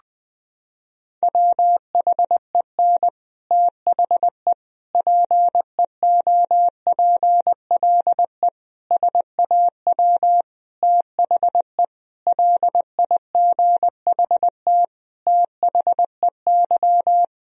Morse code, heard as a keyed tone, about 20 wpm, 700 Hz. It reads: WHEN THE PEOPLE SAW THE LIGHT THEY
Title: Sentences from Top 1000 Words - Review Part 1 - 20wpm